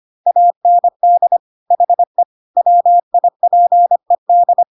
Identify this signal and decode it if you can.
Morse code, heard as a keyed tone, about 25 wpm, 700 Hz. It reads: AND HE WIPED
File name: Pinocchio / Pinocchio ch3 25 25 ik1hge